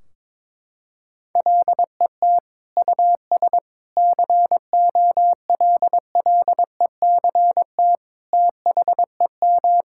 Morse code, heard as a keyed tone, about 22 wpm, 700 Hz. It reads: LET US COLLECT THEM
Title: Sentences from Top 1600 Words 22wpm